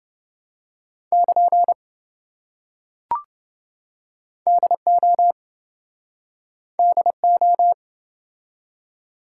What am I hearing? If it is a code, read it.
Morse code, heard as a keyed tone, about 27 wpm, 700 Hz. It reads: <KN> DO DO